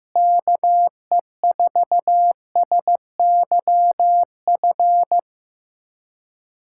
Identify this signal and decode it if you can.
Morse code, heard as a keyed tone, about 16 wpm, 700 Hz. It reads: KE4SYF
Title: USA Callsign DE Callsign - 6 Characters - 15wpm